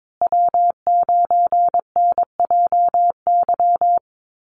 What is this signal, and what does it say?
Morse code, heard as a keyed tone, about 22 wpm, 700 Hz. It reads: W9NJY
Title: Call Signs Repeated 3x - 22wpm